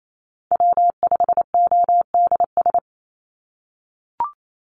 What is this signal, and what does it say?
Morse code, heard as a keyed tone, about 28 wpm, 700 Hz. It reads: W5ODS
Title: NA SOTA Chaser Callsigns 28wpm